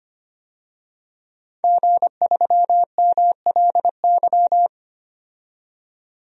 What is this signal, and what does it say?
Morse code, heard as a keyed tone, about 25 wpm, 700 Hz. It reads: G3MLY
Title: DXCC-100 Call Signs - 5 Characters - 25wpm